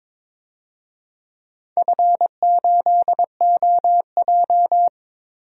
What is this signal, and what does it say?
Morse code, heard as a keyed tone, about 22 wpm, 700 Hz. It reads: F8OJ
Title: DXCC-100 Call Signs - 4 Characters - Repeated 3x - 22wpm